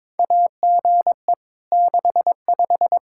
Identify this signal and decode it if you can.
Morse code, heard as a keyed tone, about 22 wpm, 700 Hz. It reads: AGE 65
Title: K1USN SST QSO Practice 22wpm